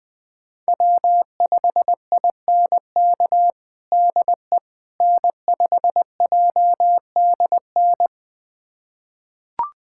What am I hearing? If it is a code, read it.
Morse code, heard as a keyed tone, about 20 wpm, 700 Hz. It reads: W5INK DE N5JDN
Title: USA Callsign DE Callsign - 5 Characters - 20wpm